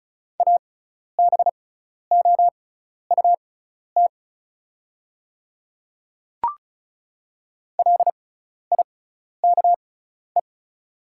Morse code, heard as a keyed tone, about 34 wpm, spaced as for 12 wpm, 690 Hz. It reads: ABOUT LIKE
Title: 6x Character Spacing - Top 100 Words - 35wpm